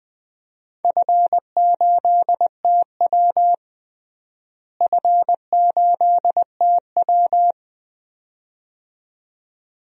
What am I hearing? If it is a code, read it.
Morse code, heard as a keyed tone, about 20 wpm, 700 Hz. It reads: F8TW F8TW
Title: DXCC-100 Call Signs - 4 Characters - Repeated 2x - 20wpm